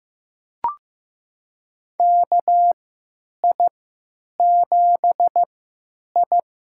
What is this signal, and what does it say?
Morse code, heard as a keyed tone, about 16 wpm, spaced as for 9 wpm, 700 Hz. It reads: KI7I
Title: USA Call Signs - 3x Character Spacing - 15wpm